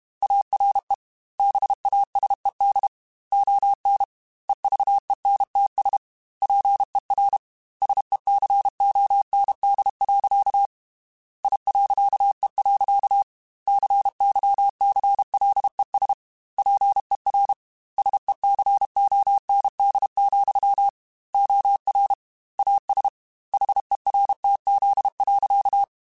Morse code, heard as a keyed tone, about 31 wpm, 800 Hz. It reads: ARE BASED ON EVENTS PER SECOND. I.E. CYCLES PER SECOND, OR AS HERTZ.
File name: Technical Subjects / Technical Subjects-mp3s / Reactance @30